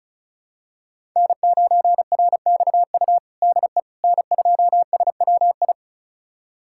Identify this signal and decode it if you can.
Morse code, heard as a keyed tone, about 34 wpm, 700 Hz. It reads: N9RXU DE N2SWI
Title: USA Callsign DE Callsign - 5 Characters - 35wpm